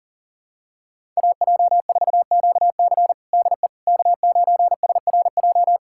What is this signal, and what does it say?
Morse code, heard as a keyed tone, about 40 wpm, 700 Hz. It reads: AJ4QC DE K9SRJ
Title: USA Callsign DE Callsign - 5 Characters - 40wpm